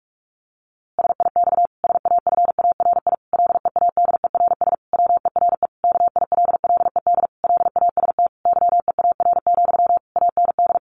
Morse code, heard as a keyed tone, about 44 wpm, 700 Hz. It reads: SIX SAFARI LEADERS WERE KILLED LAST YEAR, AND
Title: A Sound Of Thunder 45wpm